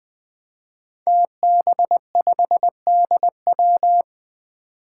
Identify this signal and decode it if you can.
Morse code, heard as a keyed tone, about 20 wpm, 700 Hz. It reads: TB5DW
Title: Five Challenging Letter-Number Combinations 20wpm